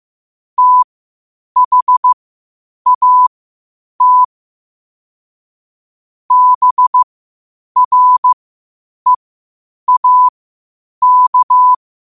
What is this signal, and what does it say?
Morse code, heard as a keyed tone, about 15 wpm, spaced as for 8 wpm, 1000 Hz. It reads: THAT BREAK
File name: qotdfiles / 8wpm